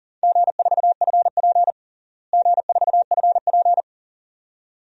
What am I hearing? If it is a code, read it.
Morse code, heard as a keyed tone, about 40 wpm, 700 Hz. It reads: G4FP G4FP
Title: DXCC-100 Call Signs - 4 Characters - Repeated 2x - 40wpm